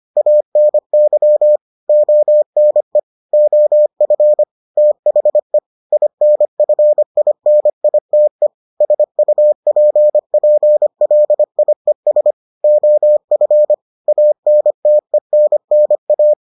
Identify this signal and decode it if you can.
Morse code, heard as a keyed tone, about 25 wpm, 600 Hz. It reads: ANY ONE OF THE INFINITE SUPPLIES OF ANTENNA